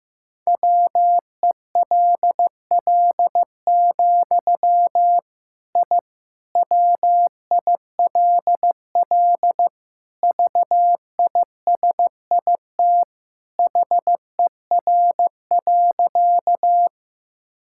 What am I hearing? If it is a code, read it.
Morse code, heard as a keyed tone, about 16 wpm, 700 Hz. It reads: WELL, I WILL VISIT HER.
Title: Sentences from Top 1000 Words - Review Part 6 - 15wpm